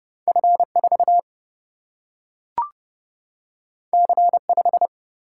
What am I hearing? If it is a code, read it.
Morse code, heard as a keyed tone, about 30 wpm, 700 Hz. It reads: F 4 C 5
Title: Single-Letter Single-Number 30wpm